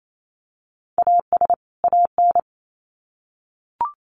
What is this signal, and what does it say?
Morse code, heard as a keyed tone, about 28 wpm, 700 Hz. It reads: AS AN
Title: Sets of 2 Words 28wpm